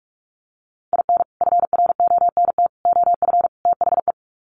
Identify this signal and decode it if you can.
Morse code, heard as a keyed tone, about 44 wpm, 700 Hz. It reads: IN FRONT OF THE
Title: Sets of 4 Words 45wpm